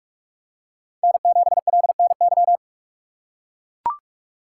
Morse code, heard as a keyed tone, about 44 wpm, 700 Hz. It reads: N7LNY